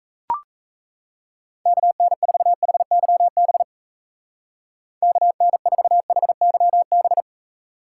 Morse code, heard as a keyed tone, about 34 wpm, 700 Hz. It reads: KN4HYB KN4HYB
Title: Speed Racing - USA 2x3 Call Signs 28wpm